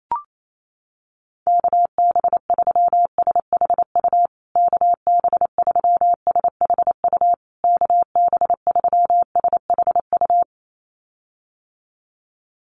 Morse code, heard as a keyed tone, about 28 wpm, 700 Hz. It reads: KB3SHU KB3SHU KB3SHU
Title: USA Call Signs Repeated 3x 28wpm